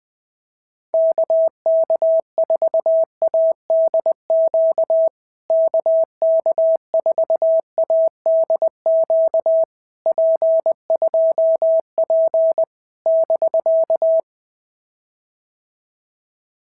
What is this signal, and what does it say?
Morse code, heard as a keyed tone, about 19 wpm, 650 Hz. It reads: KK4ADQ KK4ADQ P2P <BK>
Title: POTA QSO Practice - 20wpm